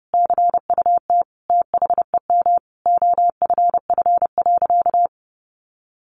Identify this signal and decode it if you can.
Morse code, heard as a keyed tone, about 30 wpm, 700 Hz. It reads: CUT THEM OFF.
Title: Sentences from Top 1000 Words 30wpm